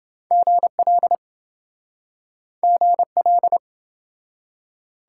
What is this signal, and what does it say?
Morse code, heard as a keyed tone, about 26 wpm, 700 Hz. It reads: GL GL